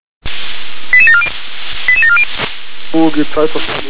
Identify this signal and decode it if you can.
Five-tone selective call (ZVEI), heard as quick runs of five tones, tones of about 70 ms, with speech. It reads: 88740 88740